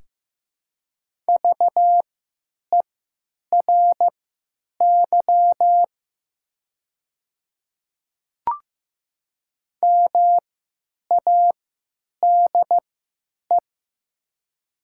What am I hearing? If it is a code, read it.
Morse code, heard as a keyed tone, about 16 wpm, spaced as for 9 wpm, 700 Hz. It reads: VERY MADE
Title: 3x Character Spacing - Top 300 Words - 15wpm